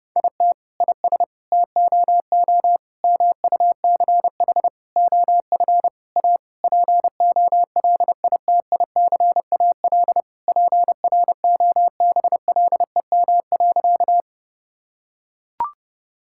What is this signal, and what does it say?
Morse code, heard as a keyed tone, about 30 wpm, 700 Hz. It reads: IT IS TOO MUCH OF A POLITICAL PROBLEM.
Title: Sentences from Top 1000 Words - Review Part 5 - 30wpm